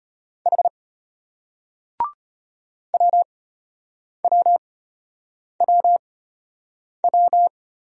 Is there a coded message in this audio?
Yes: Morse code from 0.5 to 7.5 s, heard as a keyed tone, about 33 wpm, spaced as for 8 wpm, 700 Hz. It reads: H WWWW